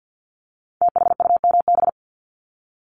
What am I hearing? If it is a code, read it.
Morse code, heard as a keyed tone, about 50 wpm, 690 Hz. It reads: THUMB
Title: Multi-letter Phonemes 50wpm